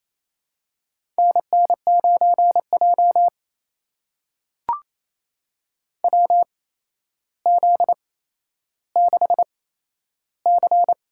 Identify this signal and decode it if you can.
Morse code, heard as a keyed tone, about 28 wpm, spaced as for 12 wpm, 700 Hz. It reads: NN9J WZ6C
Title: USA Call Signs - 4 Characters - 8x Character Spacing - 28wpm